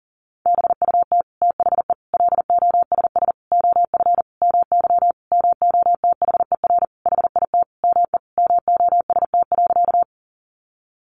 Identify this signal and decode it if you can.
Morse code, heard as a keyed tone, about 40 wpm, 700 Hz. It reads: BUT THE LOSS OF MY MOTHER HIT ME MOST.
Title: Sentences from Top 1400 Words 40wpm